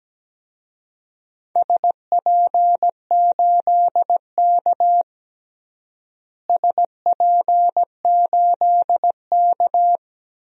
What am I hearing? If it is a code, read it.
Morse code, heard as a keyed tone, about 17 wpm, 700 Hz. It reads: SP8K SP8K
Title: DXCC-100 Call Signs - 4 Characters - Repeated 2x - 17wpm